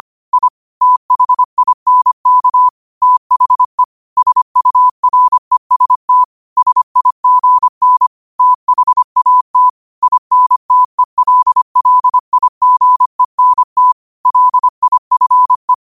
Morse code, heard as a keyed tone, about 25 wpm, 1000 Hz. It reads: I THINK THE SUREST SIGN THAT INTELLIGENT LIFE
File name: qotdfiles / 25wpm